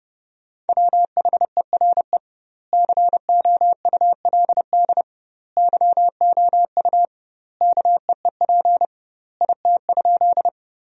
Morse code, heard as a keyed tone, about 30 wpm, 700 Hz. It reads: WHERE COULD YOU KEEP IT?
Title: 2x Word-Spacing - Sentences from Top 300 Words - 30wpm